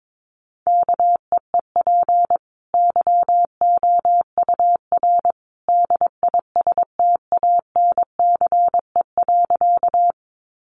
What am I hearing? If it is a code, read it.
Morse code, heard as a keyed tone, about 22 wpm, 700 Hz. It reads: KEEP YOUR DISTANCE.
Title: Sentences from Top 1200 Words 22wpm